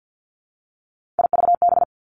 Morse code, heard as a keyed tone, about 50 wpm, 700 Hz. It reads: IVB
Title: Three Challenging Letter-Number Combinations 50wpm